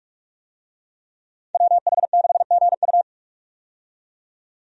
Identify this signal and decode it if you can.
Morse code, heard as a keyed tone, about 46 wpm, 700 Hz. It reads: WH6GU
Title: USA Call Signs - 5 Characters - 45wpm